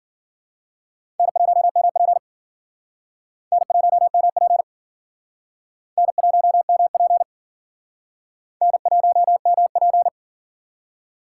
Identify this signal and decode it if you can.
Morse code, heard as a keyed tone, about 53 wpm, 700 Hz. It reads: N1MP N1MP N1MP N1MP